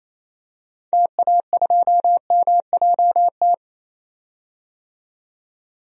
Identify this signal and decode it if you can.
Morse code, heard as a keyed tone, about 28 wpm, 700 Hz. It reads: TA2MJT
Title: DXCC-100 Call Signs - 6 Characters - 28wpm